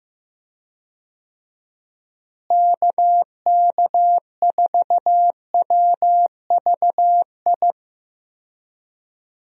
Morse code, heard as a keyed tone, about 16 wpm, 700 Hz. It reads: KK4WVI
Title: USA Call Signs - 6 Characters - Repeated 3x - 15wpm